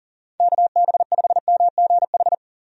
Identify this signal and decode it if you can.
Morse code, heard as a keyed tone, about 40 wpm, 700 Hz. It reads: KB5MGH
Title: USA Call Signs - 6 Characters - 40wpm